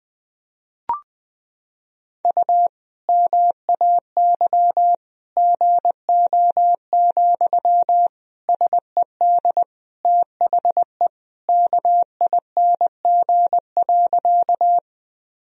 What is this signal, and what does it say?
Morse code, heard as a keyed tone, about 20 wpm, 700 Hz. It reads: U MAY GO, SED THE KING.
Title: Sentences from Top 400 Words and Abbreviations 20wpm